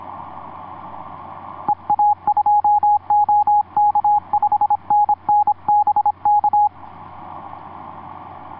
Morse code, heard as a keyed tone, about 26 wpm, 830 Hz. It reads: EA2OK 5NN BK
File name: zs2dl ea2ok etc 12m cw 07mar2010 1637